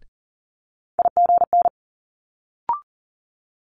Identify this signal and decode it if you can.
Morse code, heard as a keyed tone, about 38 wpm, 700 Hz. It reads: IGN
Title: Multi-letter Phonemes Quick Fire 40wpm